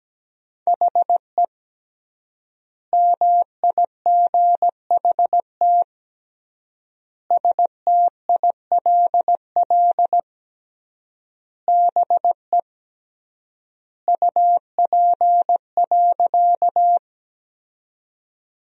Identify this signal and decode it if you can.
Morse code, heard as a keyed tone, about 17 wpm, 700 Hz. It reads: HE MIGHT STILL BE UP.
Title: 3x Word-Spacing - Sentences from Top 300 Words - 17wpm